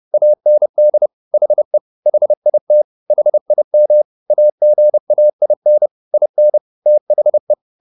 Morse code, heard as a keyed tone, about 30 wpm, 600 Hz. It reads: AND HE HIT HIM AGAIN IN THE